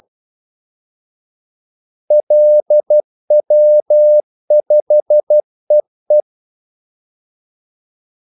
Morse code, heard as a keyed tone, about 12 wpm, 600 Hz. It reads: LW5EE
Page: 2 ekstra ordmellemrum mellem kaldesignalerne